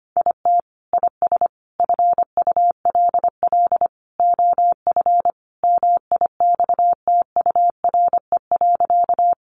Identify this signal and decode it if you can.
Morse code, heard as a keyed tone, about 25 wpm, 700 Hz. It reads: IT IS FULL OF MIXTURE.
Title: Sentences from Top 1500 Words 25wpm